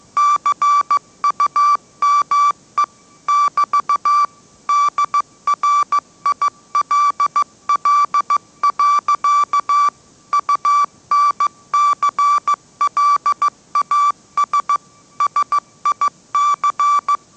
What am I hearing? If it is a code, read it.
Morse code, heard as a keyed tone, about 15 wpm, 1160 Hz. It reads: CUME <BT> DRILL. UNCLAS SIC
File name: mrx / mrx066 16wpm